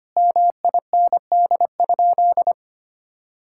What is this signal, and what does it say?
Morse code, heard as a keyed tone, about 25 wpm, 700 Hz. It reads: MIND?
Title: Sentences from Top 1000 Words - Review Part 5 - 25wpm